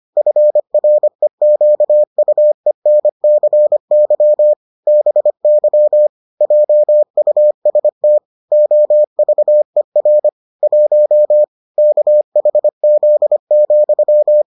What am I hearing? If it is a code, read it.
Morse code, heard as a keyed tone, about 25 wpm, 600 Hz. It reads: FREQUENCY BY JUST OVER 1 KHZ,